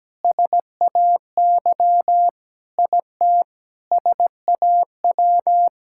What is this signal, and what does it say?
Morse code, heard as a keyed tone, about 17 wpm, 700 Hz. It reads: SAY IT SAW
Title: Sentences from Top 1000 Words - Review Part 5 - 17wpm